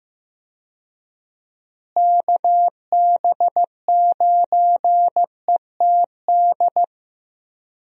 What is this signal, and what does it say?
Morse code, heard as a keyed tone, about 15 wpm, 700 Hz. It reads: KB9ETD
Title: USA Call Signs - 6 Characters - Repeated 3x - 15wpm